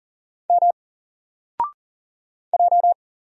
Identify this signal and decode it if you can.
Morse code, heard as a keyed tone, about 40 wpm, 700 Hz. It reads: M J
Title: Single Letter-Number 40wpm